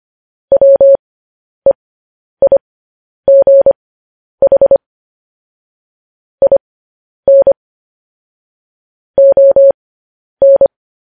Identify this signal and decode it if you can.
Morse code, heard as a keyed tone, about 25 wpm, spaced as for 10 wpm, 550 Hz. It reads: WEIGH IN ON
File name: podcast / News headlines-1x-25x10